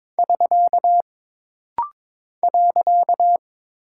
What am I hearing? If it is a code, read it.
Morse code, heard as a keyed tone, about 22 wpm, 700 Hz. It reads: <SK> .
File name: practice / 22wpm / Single Punctuation-Prosign - Rapid-Fire - 22wpm